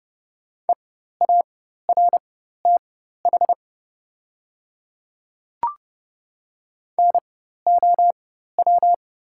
Morse code, heard as a keyed tone, about 31 wpm, spaced as for 14 wpm, 700 Hz. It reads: EARTH NOW
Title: 4x Character Spacing - Top 200 Words - 30wpm